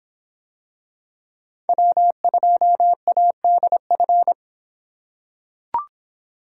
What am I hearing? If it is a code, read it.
Morse code, heard as a keyed tone, about 26 wpm, 700 Hz. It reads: W2ADF